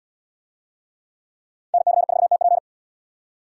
Morse code, heard as a keyed tone, about 68 wpm, 700 Hz. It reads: NC4TP